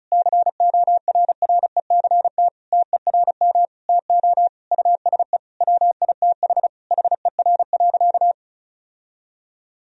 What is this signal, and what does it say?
Morse code, heard as a keyed tone, about 36 wpm, 700 Hz. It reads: CORRECT TERM TO USE WITH HER.
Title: Sentences from Top 1000 Words - Review Part 4 - 35wpm